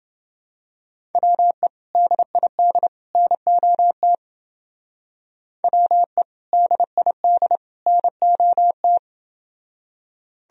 Morse code, heard as a keyed tone, about 29 wpm, 700 Hz. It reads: WE DID NOT WE DID NOT